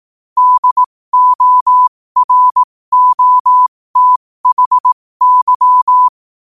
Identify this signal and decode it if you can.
Morse code, heard as a keyed tone, about 18 wpm, spaced as for 15 wpm, 1000 Hz. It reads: DOROTHY
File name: qotdfiles / 15wpm